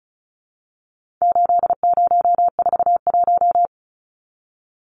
Morse code, heard as a keyed tone, about 34 wpm, 700 Hz. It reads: 8041
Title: Numbers - 4-Digit 35wpm